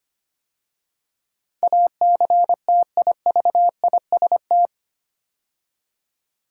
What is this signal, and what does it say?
Morse code, heard as a keyed tone, about 25 wpm, 700 Hz. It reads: ACTIVIST
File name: practice / 25wpm / 8 Letter Words - First 3 Letters Given - 25wpm